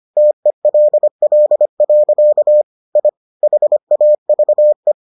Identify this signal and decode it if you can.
Morse code, heard as a keyed tone, about 25 wpm, 600 Hz. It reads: TELL. I HAVE